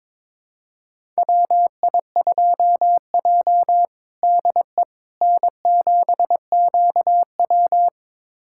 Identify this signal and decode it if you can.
Morse code, heard as a keyed tone, about 22 wpm, 700 Hz. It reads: WI2J DE N7QW
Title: USA Callsign DE Callsign - 4 Characters - 22wpm